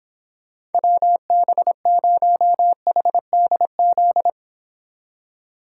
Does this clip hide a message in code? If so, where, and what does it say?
Yes: Morse code from 0.7 to 4.3 s, heard as a keyed tone, about 31 wpm, 700 Hz. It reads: WB0HDZ